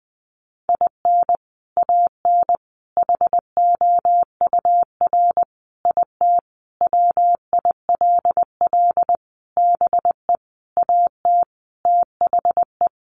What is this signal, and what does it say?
Morse code, heard as a keyed tone, about 20 wpm, 700 Hz. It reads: IN AN HOUR IT WILL BE AT THE
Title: Sentences from Top 400 Words 20wpm